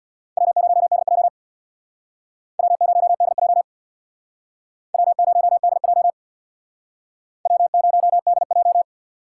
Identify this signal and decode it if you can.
Morse code, heard as a keyed tone, about 61 wpm, 700 Hz. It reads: W0DJ W0DJ W0DJ W0DJ
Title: Speed Racing - USA 1x2 Call Signs 50wpm